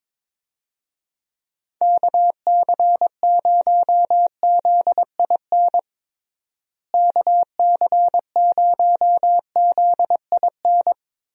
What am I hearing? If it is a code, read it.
Morse code, heard as a keyed tone, about 22 wpm, 700 Hz. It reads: KC0ZIN KC0ZIN
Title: USA Call Signs - 6 Characters - Repeated 3x - 22wpm